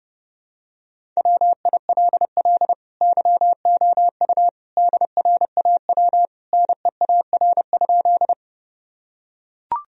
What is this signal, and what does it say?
Morse code, heard as a keyed tone, about 30 wpm, 700 Hz. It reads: WILL YOU DRAW NEAR?
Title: Sentences from Top 1000 Words - Review Part 1 - 30wpm